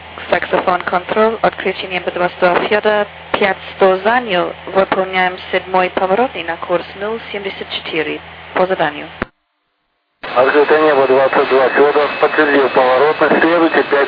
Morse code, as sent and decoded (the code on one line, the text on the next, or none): none